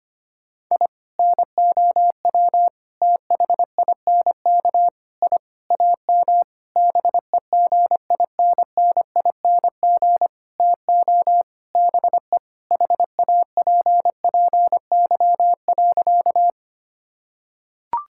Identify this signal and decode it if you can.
Morse code, heard as a keyed tone, about 25 wpm, 700 Hz. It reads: I NOW THINK I AM BEGINNING TO BE HAPPY.
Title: Sentences from Top 1000 Words - Review Part 5 - 25wpm